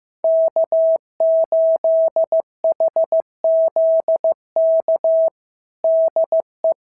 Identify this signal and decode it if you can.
Morse code, heard as a keyed tone, about 15 wpm, 650 Hz. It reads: K8HZK DE
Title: SOTA QSO Practice 15wpm